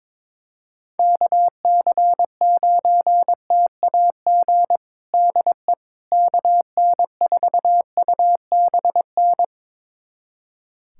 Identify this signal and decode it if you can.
Morse code, heard as a keyed tone, about 22 wpm, 700 Hz. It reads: KC9TAG DE KN4UBN
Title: USA Callsign DE Callsign - 6 Characters - 22wpm